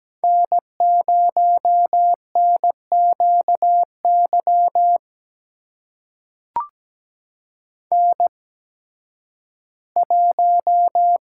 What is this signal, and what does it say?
Morse code, heard as a keyed tone, about 17 wpm, spaced as for 11 wpm, 700 Hz. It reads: N0NQY N1
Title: USA Call Signs - 5 Characters - 8x Character Spacing - 17wpm